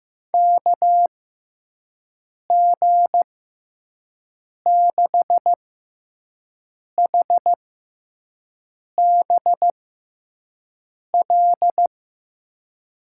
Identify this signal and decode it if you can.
Morse code, heard as a keyed tone, about 15 wpm, spaced as for 5 wpm, 700 Hz. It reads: KG6HBL
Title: USA Call Signs - 6x Character Spacing - 15wpm